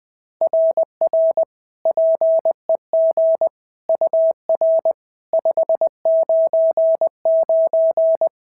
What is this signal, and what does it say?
Morse code, heard as a keyed tone, about 19 wpm, 650 Hz. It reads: RR PEG UR 599
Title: SOTA QSO Practice 20wpm